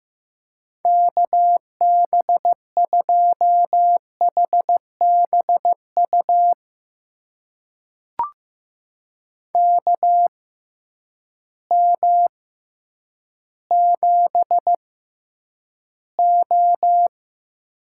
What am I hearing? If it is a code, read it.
Morse code, heard as a keyed tone, about 15 wpm, spaced as for 9 wpm, 700 Hz. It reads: KB2HBU KM7O
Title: USA Call Signs - 6x Character Spacing - 15wpm